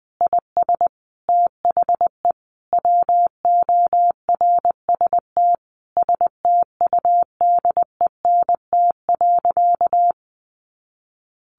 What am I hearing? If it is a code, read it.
Morse code, heard as a keyed tone, about 20 wpm, 700 Hz. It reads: IS THE WORST STUDENT.
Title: Additional Easy Phrases 20wpm